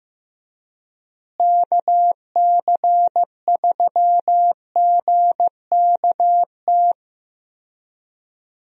Morse code, heard as a keyed tone, about 15 wpm, 700 Hz. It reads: KC3GKT
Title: USA Call Signs - 6 Characters - Repeated 3x - 15wpm